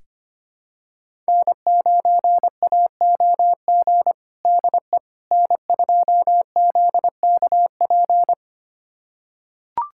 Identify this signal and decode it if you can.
Morse code, heard as a keyed tone, about 25 wpm, 700 Hz. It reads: N9AOG DE N2ZKP
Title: USA Callsign DE Callsign - 5 Characters - 25wpm